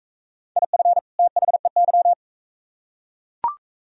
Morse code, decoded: IF THEY

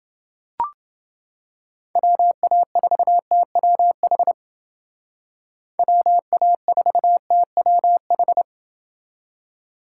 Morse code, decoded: WA4TWH WA4TWH

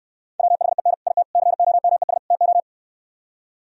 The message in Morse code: WHEN IT COMES TO

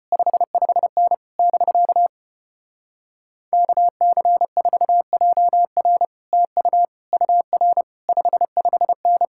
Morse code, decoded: 55N <BK> KC4JR TU UR 55N

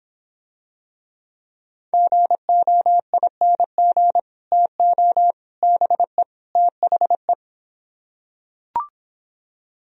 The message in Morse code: GOING TO BE THE

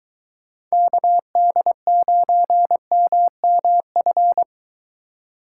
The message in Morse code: KD9MMF

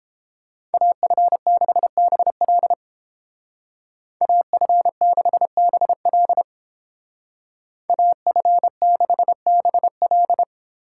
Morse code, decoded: AF6BL AF6BL AF6BL

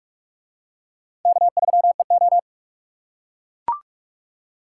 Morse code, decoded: K3EO

